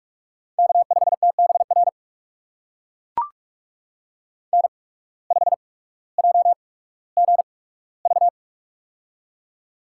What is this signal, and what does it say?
Morse code, heard as a keyed tone, about 44 wpm, spaced as for 19 wpm, 700 Hz. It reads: K5TBR N5JGV
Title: USA Call Signs - 8x Character Spacing - 45wpm